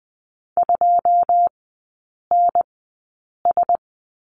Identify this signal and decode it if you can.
Morse code, heard as a keyed tone, about 20 wpm, spaced as for 8 wpm, 700 Hz. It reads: 2NS
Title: 114 Lesson 19e - Letter w - Three Characters 20wpm